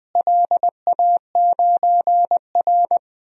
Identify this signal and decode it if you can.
Morse code, heard as a keyed tone, about 20 wpm, 700 Hz. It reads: LA9R
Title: DXCC-100 Call Signs - 4 Characters - Repeated 2x - 20wpm